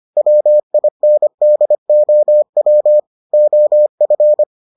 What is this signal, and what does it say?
Morse code, heard as a keyed tone, about 25 wpm, 600 Hz. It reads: WINDOW OF